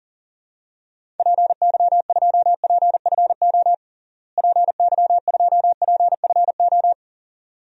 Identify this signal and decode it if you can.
Morse code, heard as a keyed tone, about 37 wpm, 700 Hz. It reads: PY2PFO PY2PFO
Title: DXCC-100 Call Signs - 6 Characters - Repeated 2x - 40wpm